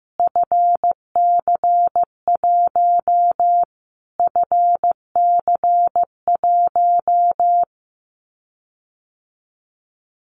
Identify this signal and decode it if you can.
Morse code, heard as a keyed tone, about 15 wpm, 700 Hz. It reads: FC1 FC1
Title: Callsign Prefixes for Top 25 DXCC - Repeated 3x - 15wpm